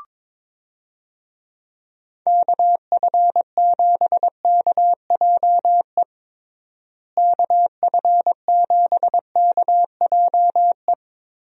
Morse code, heard as a keyed tone, about 22 wpm, 700 Hz. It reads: KF7KJE KF7KJE
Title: USA Call Signs - 6 Characters - Repeated 2x - 22wpm